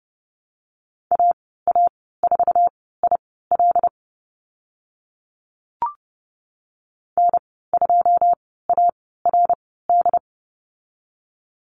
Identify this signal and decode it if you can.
Morse code, heard as a keyed tone, about 30 wpm, spaced as for 17 wpm, 700 Hz. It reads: AA4IL N2ARD
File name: Lesson 14g - Number 2 - Call Signs 30wpm